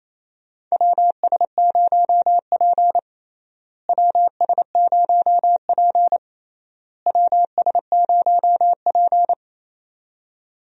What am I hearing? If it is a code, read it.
Morse code, heard as a keyed tone, about 28 wpm, 700 Hz. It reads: WS0P WS0P WS0P
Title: USA Call Signs - 4 Characters - Repeated 3x - 28wpm